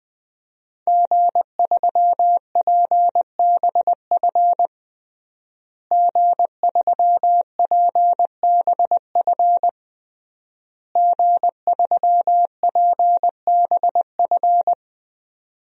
Morse code, heard as a keyed tone, about 20 wpm, 700 Hz. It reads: G3PBF G3PBF G3PBF
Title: DXCC-100 Call Signs - 5 Characters - Repeated 3x - 20wpm